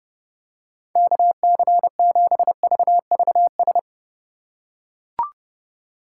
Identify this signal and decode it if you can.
Morse code, heard as a keyed tone, about 30 wpm, 700 Hz. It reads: KC7VVS